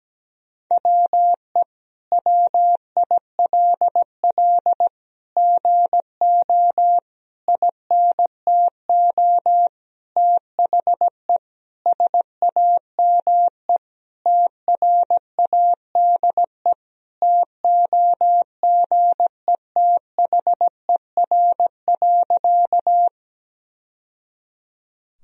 Morse code, heard as a keyed tone, about 17 wpm, 700 Hz. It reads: WE WILL GO INTO THE SAME TRADE TOGETHER.
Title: Sentences from Top 1000 Words - Review Part 4 - 17wpm